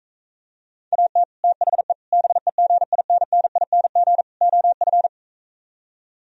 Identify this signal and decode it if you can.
Morse code, heard as a keyed tone, about 45 wpm, 700 Hz. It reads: AT THE BEGINNING OF